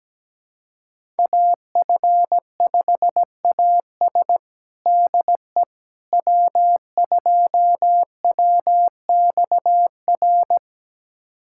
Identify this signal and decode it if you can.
Morse code, heard as a keyed tone, about 17 wpm, 700 Hz. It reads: AF5AS DE W2WXR
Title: USA Callsign DE Callsign - 5 Characters - 17wpm